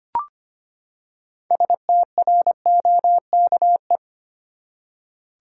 Morse code, heard as a keyed tone, about 25 wpm, 700 Hz. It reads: STROKE
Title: Six Letter Words - 25wpm